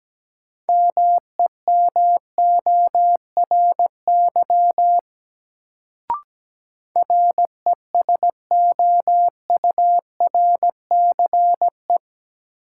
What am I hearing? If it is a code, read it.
Morse code, heard as a keyed tone, about 17 wpm, 700 Hz. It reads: MEMORY RESOURCE
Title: Sentences from Top 1300 Words - New Word Drill Alt - 17wpm